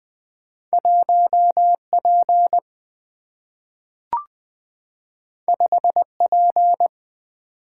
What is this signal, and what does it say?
Morse code, heard as a keyed tone, about 20 wpm, 700 Hz. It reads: 1 P 5 P
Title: Single-Letter Single-Number 20wpm